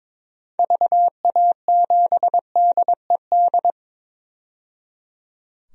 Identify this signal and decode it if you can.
Morse code, heard as a keyed tone, about 22 wpm, 700 Hz. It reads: VA7DED
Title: DXCC-100 Call Signs - 6 Characters - 22wpm